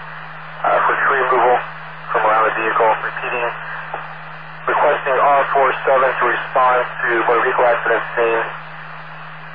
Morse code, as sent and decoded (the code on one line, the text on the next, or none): none